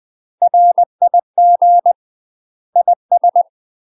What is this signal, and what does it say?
Morse code, heard as a keyed tone, about 20 wpm, 700 Hz. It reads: RIG IS
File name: UK MASTER QSOs TRAINING / 20WPM / UK MASTER QSOs 0040